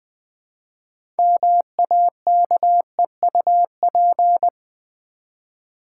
Morse code, heard as a keyed tone, about 21 wpm, 700 Hz. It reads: MAKEUP
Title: 6 Letter Words - First 3 Letters Given - 20wpm